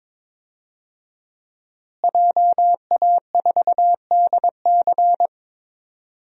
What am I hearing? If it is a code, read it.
Morse code, heard as a keyed tone, about 22 wpm, 700 Hz. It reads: JA4DC
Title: DXCC-100 Call Signs - 5 Characters - Repeated 2x - 22wpm